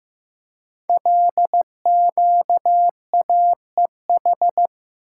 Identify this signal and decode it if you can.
Morse code, heard as a keyed tone, about 15 wpm, 700 Hz. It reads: LQAEH
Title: Code Groups - 5 Letters - 15wpm